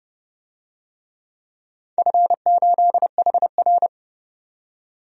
Morse code, heard as a keyed tone, about 30 wpm, 700 Hz. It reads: F8HR
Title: DXCC-100 Call Signs - 4 Characters - 30wpm